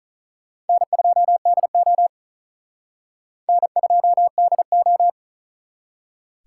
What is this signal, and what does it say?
Morse code, heard as a keyed tone, about 46 wpm, 700 Hz. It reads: N2DO N2DO